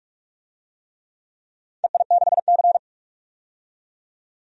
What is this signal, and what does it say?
Morse code, heard as a keyed tone, about 44 wpm, 690 Hz. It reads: EI6C
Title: DXCC-100 Call Signs - 4 Characters - 45wpm